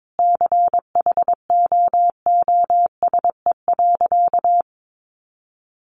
Morse code, heard as a keyed tone, about 22 wpm, 700 Hz. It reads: CHOOSE.
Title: Sentences from Top 700 Words 22wpm